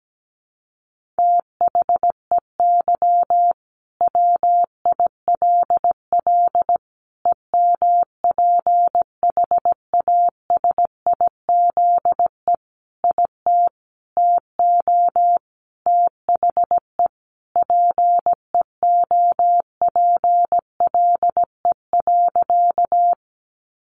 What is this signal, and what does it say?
Morse code, heard as a keyed tone, about 17 wpm, 700 Hz. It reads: THEY WILL EMPHASIZE IT TO THE PEOPLE.
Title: Sentences from Top 1500 Words 17wpm